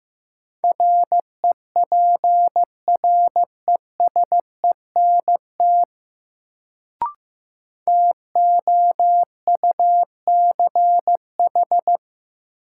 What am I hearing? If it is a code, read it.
Morse code, heard as a keyed tone, about 15 wpm, 700 Hz. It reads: REPRESENT TOUCH
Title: Sentences from Top 1200 Words - New Word Drill - 15wpm